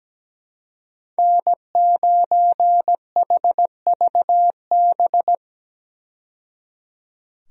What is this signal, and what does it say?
Morse code, heard as a keyed tone, about 17 wpm, 700 Hz. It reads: N9HVB
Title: USA Call Signs - 5 Characters - 17wpm